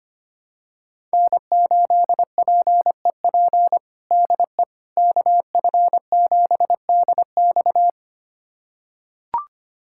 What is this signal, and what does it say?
Morse code, heard as a keyed tone, about 25 wpm, 700 Hz. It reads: N8PEP DE KF7DX
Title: USA Callsign DE Callsign - 5 Characters - 25wpm